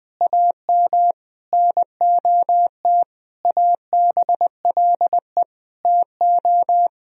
Morse code, heard as a keyed tone, about 20 wpm, 700 Hz. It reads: AM NOT ABLE TO
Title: Sentences from Top 1000 Words - Review Part 2 - 20wpm